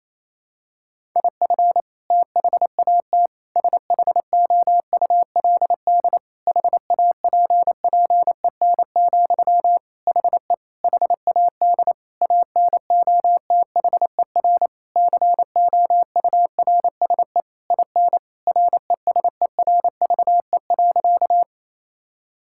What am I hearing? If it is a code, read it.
Morse code, heard as a keyed tone, about 28 wpm, 700 Hz. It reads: IF THAT SHOULD HAPPEN, HE HAD ANOTHER COURSE IN RESERVE.
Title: Sentences from Top 1000 Words - Review Part 6 - 28wpm